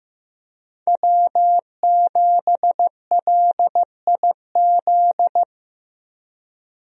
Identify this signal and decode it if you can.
Morse code, heard as a keyed tone, about 15 wpm, 700 Hz. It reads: W7LIZ
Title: USA Call Signs - 5 Characters - Repeated 2x - 15wpm